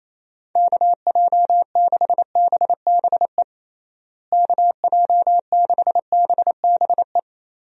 Morse code, heard as a keyed tone, about 28 wpm, 700 Hz. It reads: KJ6BBE KJ6BBE
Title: USA Call Signs - 6 Characters - Repeated 3x - 28wpm